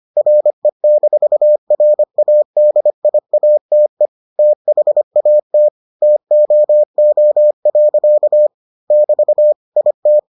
Morse code, heard as a keyed tone, about 25 wpm, 600 Hz. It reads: RE-RADIATE THAT TOO. <BT> IT